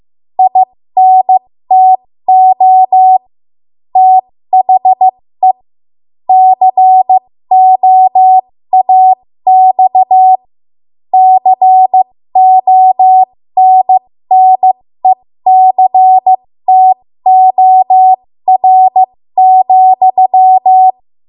Morse code, heard as a keyed tone, about 15 wpm, spaced as for 13 wpm, 750 Hz. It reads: INTO THE COAX CONNECTOR,
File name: Morse / 13WPM